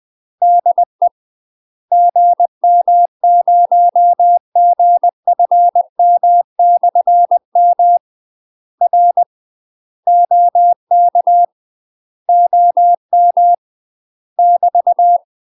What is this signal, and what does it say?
Morse code, heard as a keyed tone, about 20 wpm, 700 Hz. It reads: DE GM0GFM/M R OK OM <BT>
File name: UK MASTER QSOs TRAINING / 20WPM / UK MASTER QSOs 0010